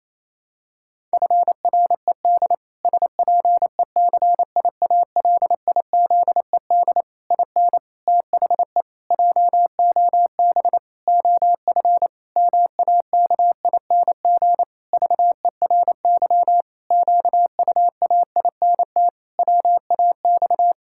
Morse code, heard as a keyed tone, about 28 wpm, 700 Hz. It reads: FRED SPECIALIZED IN THE JOB OF MAKING VERY QUAINT WAX
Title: Pangram Challenge v2 - 28wpm